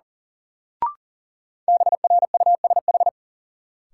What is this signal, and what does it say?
Morse code, heard as a keyed tone, about 37 wpm, 700 Hz. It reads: BRUSH
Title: Sentences from Top 1100 Words - New Word Drill Alt - 40wpm